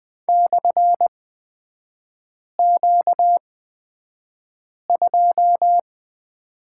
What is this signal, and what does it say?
Morse code, heard as a keyed tone, about 20 wpm, spaced as for 5 wpm, 700 Hz. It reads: /Q2
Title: 236 Lesson 38e - Letter q - Three Characters 20wpm